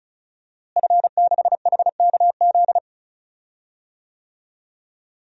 Morse code, heard as a keyed tone, about 36 wpm, 700 Hz. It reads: F6HKZ
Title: DXCC-100 Call Signs - 5 Characters - Repeated 2x - 35wpm